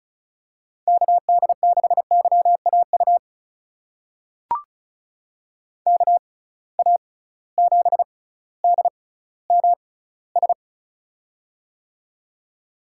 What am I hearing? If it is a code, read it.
Morse code, heard as a keyed tone, about 36 wpm, spaced as for 18 wpm, 700 Hz. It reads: KD6YAU KA7DMS
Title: USA Call Signs - 6 Characters - 6x Character Spacing - 35wpm